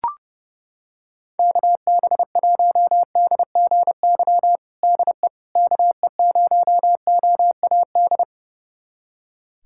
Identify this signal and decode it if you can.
Morse code, heard as a keyed tone, about 32 wpm, 700 Hz. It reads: KB1DGY DE KE0OAD
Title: USA Callsign DE Callsign - 6 Characters - 30wpm